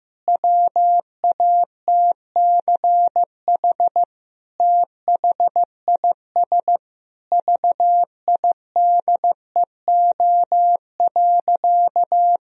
Morse code, heard as a keyed tone, about 16 wpm, 700 Hz. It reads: WATCH THIS VIDEO.
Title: Sentences from Top 1000 Words - Review Part 4 - 15wpm